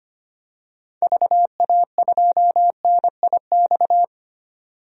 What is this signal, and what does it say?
Morse code, heard as a keyed tone, about 25 wpm, 700 Hz. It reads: VA2NIX
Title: DXCC-100 Call Signs - 6 Characters - 25wpm